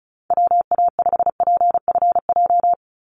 Morse code, heard as a keyed tone, about 34 wpm, 700 Hz. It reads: WA5PFJ
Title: Call Signs Repeated 2x - 35wpm